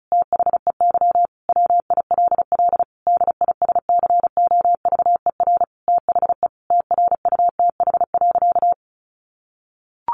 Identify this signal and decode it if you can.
Morse code, heard as a keyed tone, about 34 wpm, 700 Hz. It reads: THEY WILL DISCOVER THE TRUTH.
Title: Sentences from Top 1100 Words 35wpm